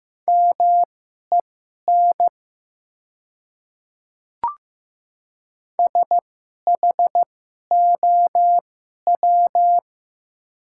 Morse code, heard as a keyed tone, about 15 wpm, spaced as for 11 wpm, 700 Hz. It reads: MEN SHOW
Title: 2x Character Spacing - Top 300 Words - 15wpm